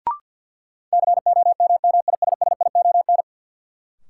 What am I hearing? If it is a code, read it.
Morse code, heard as a keyed tone, about 46 wpm, 700 Hz. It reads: COMMISSION
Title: Sentences from Top 1400 Words - New Word Drill - 50wpm